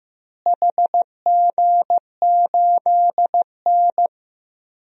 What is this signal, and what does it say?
Morse code, heard as a keyed tone, about 15 wpm, 700 Hz. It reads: HG8N
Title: DXCC-100 Call Signs - 4 Characters - Repeated 2x - 15wpm